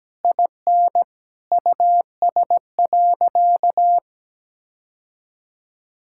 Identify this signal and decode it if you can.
Morse code, heard as a keyed tone, about 17 wpm, 700 Hz. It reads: IN US.
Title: Sentences from Top 1000 Words - Review Part 6 - 17wpm